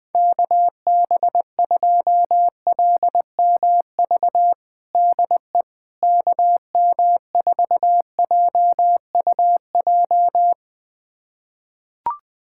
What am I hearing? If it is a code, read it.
Morse code, heard as a keyed tone, about 20 wpm, 700 Hz. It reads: KB2LMV DE KM4JUJ
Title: USA Callsign DE Callsign - 6 Characters - 20wpm